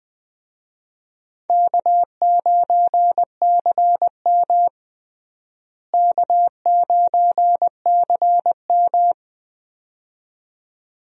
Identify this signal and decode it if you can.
Morse code, heard as a keyed tone, about 19 wpm, 700 Hz. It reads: K9CM K9CM
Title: USA Call Signs - 4 Characters - Repeated 2x - 20wpm